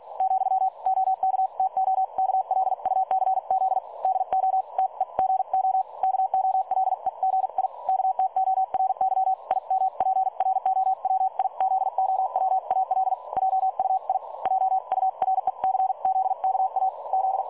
Morse code, heard as a keyed tone, about 48 wpm, spaced as for 41 wpm, 760 Hz. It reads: , OK TO KLUCZ DO TEGO KOLEGI KTORY IMPROWIZOWAL JUI JURECZKA Z
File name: dlazaawansowanych / nagraniazpasma / websdr recording 3565.0kHz